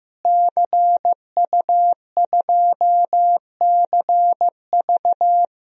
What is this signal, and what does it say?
Morse code, heard as a keyed tone, about 15 wpm, 700 Hz. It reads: CU2CV
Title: DXCC-100 Call Signs - 5 Characters - Repeated 3x - 15wpm